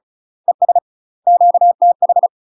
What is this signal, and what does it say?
Morse code, heard as a keyed tone, about 36 wpm, 700 Hz. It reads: ES QTH
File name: UK MASTER QSOs TRAINING / 35WPM / UK MASTER QSOs 0023